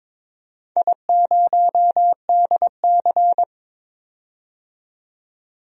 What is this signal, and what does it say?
Morse code, heard as a keyed tone, about 22 wpm, 700 Hz. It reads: I0DC
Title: DXCC-100 Call Signs - 4 Characters - 22wpm